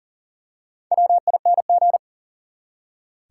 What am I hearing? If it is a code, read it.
Morse code, heard as a keyed tone, about 38 wpm, 700 Hz. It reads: WING
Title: Four Letter Words - 40wpm